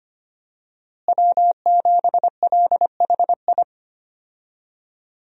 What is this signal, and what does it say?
Morse code, heard as a keyed tone, about 25 wpm, 700 Hz. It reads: W7LHI
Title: USA Call Signs - 5 Characters - Repeated 3x - 25wpm